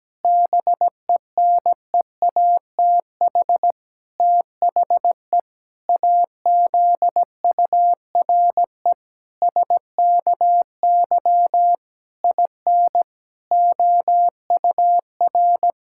Morse code, heard as a keyed tone, about 17 wpm, 700 Hz. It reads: BENEATH THE AZURE SKY IN OUR
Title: Pangram Challenge v2 - 17wpm